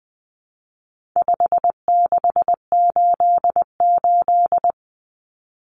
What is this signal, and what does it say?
Morse code, heard as a keyed tone, about 20 wpm, 700 Hz. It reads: 5688
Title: Common Cut Numbers - 4-Digit 20wpm